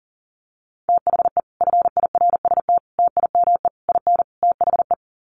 Morse code, heard as a keyed tone, about 40 wpm, 700 Hz. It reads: THE FIRST TIME IN THE
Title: Sets of 5 Words Encore 40wpm